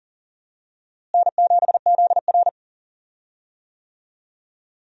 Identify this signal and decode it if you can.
Morse code, heard as a keyed tone, about 40 wpm, 700 Hz. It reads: N7ZR
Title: USA Call Signs - 4 Characters - Repeated 2x - 40wpm